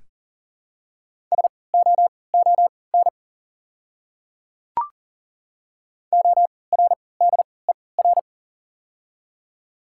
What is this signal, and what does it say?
Morse code, heard as a keyed tone, about 39 wpm, spaced as for 22 wpm, 690 Hz. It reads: SOON ORDER
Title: 3x Character Spacing - Top 500 Words - 40wpm